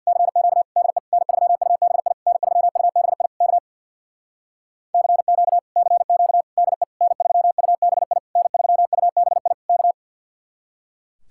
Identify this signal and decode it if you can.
Morse code, heard as a keyed tone, about 66 wpm, 700 Hz. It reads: CQ DE N3UBI N3UBI K CQ CQ DE N3UBI N3UBI K